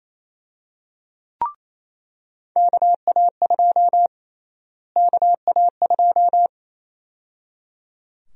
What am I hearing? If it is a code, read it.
Morse code, heard as a keyed tone, about 28 wpm, 700 Hz. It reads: KA2 KA2
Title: USA Call Sign Prefixes - Repeated 2x - 28wpm